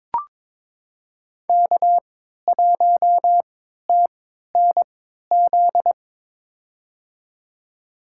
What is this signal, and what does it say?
Morse code, heard as a keyed tone, about 22 wpm, spaced as for 13 wpm, 700 Hz. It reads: K1TNZ
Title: USA Call Signs - 5 Characters - 3x Character Spacing - 22wpm